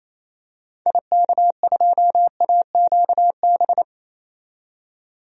Morse code, heard as a keyed tone, about 28 wpm, 700 Hz. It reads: IK2AQB
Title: DXCC-100 Call Signs - 6 Characters - 28wpm